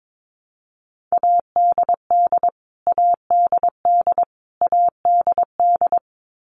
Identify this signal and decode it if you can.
Morse code, heard as a keyed tone, about 22 wpm, 700 Hz. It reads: ADD ADD ADD
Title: Top 500 Words Repeated 22wpm